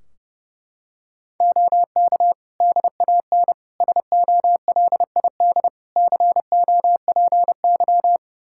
Morse code, heard as a keyed tone, about 30 wpm, 700 Hz. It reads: OK DAN SOLID COPY
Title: Practice QSO - Exchange 3 - More Info - 30wpm